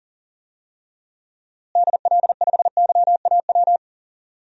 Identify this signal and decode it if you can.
Morse code, heard as a keyed tone, about 37 wpm, 700 Hz. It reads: DL5YAW
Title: DXCC-100 Call Signs - 6 Characters - Repeated 3x - 40wpm